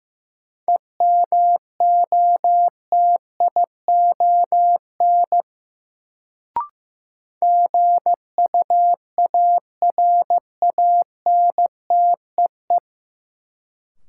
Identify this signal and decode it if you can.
Morse code, heard as a keyed tone, about 15 wpm, 700 Hz. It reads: EMOTION GUARANTEE
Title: Sentences from Top 1600 Words - New Word Drill Alt - 15wpm